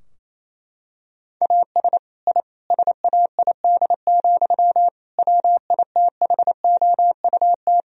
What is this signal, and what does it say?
Morse code, heard as a keyed tone, about 28 wpm, 700 Hz. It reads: AS I SAID, WITHOUT
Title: Sentences from Top 1000 Words - Review Part 3 - 28wpm